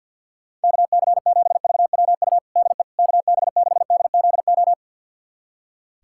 Morse code, heard as a keyed tone, about 47 wpm, 700 Hz. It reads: KC7VWU DE KB6DZO
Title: USA Callsign DE Callsign 50wpm